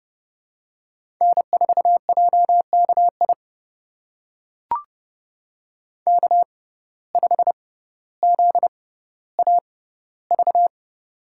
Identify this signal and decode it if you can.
Morse code, heard as a keyed tone, about 30 wpm, spaced as for 15 wpm, 700 Hz. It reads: N4JKI K5ZAV
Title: USA Call Signs - 5 Characters - 6x Character Spacing - 30wpm